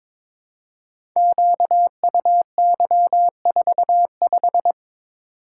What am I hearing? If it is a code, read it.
Morse code, heard as a keyed tone, about 22 wpm, 700 Hz. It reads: QUY45
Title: Code Groups - 5 Letter:Numbers - 22wpm